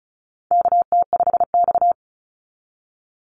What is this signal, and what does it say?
Morse code, heard as a keyed tone, about 36 wpm, 700 Hz. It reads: KT5X
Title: NA SOTA Chaser Callsigns 35wpm